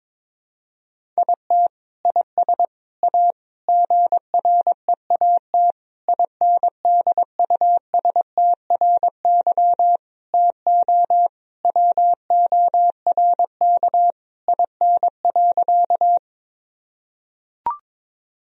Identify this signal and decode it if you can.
Morse code, heard as a keyed tone, about 22 wpm, 700 Hz. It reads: IT IS A GREAT INDUSTRY TO WORK IN.
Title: Sentences from Top 1000 Words - Review Part 4 - 22wpm